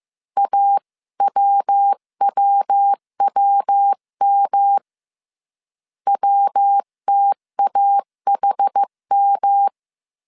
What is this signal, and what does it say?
Morse code, heard as a keyed tone, about 15 wpm, 800 Hz. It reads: AWWWM WTAHM
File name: CW / DET / 1 3 1500C S C 14wpm 2458